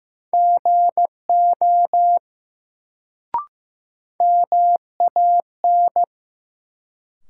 Morse code, heard as a keyed tone, about 15 wpm, 700 Hz. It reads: GO MAN
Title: Sets of 2 Words - New Word Drill - 15wpm